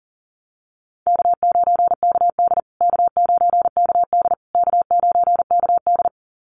Morse code, heard as a keyed tone, about 40 wpm, 700 Hz. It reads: K9KD K9KD K9KD
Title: USA Call Signs Repeated 3x 40wpm